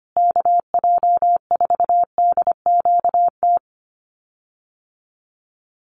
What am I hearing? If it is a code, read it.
Morse code, heard as a keyed tone, about 25 wpm, 700 Hz. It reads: KJ4DQT
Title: USA Call Signs Repeated 3x 25wpm